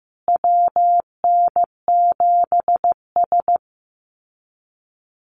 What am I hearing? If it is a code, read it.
Morse code, heard as a keyed tone, about 15 wpm, 700 Hz. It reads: WN7S
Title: Call Signs Repeated 2x - 15wpm